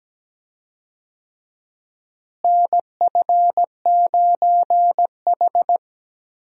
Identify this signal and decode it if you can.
Morse code, heard as a keyed tone, about 17 wpm, 700 Hz. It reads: NF9H
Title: USA Call Signs - 4 Characters - Repeated 2x - 17wpm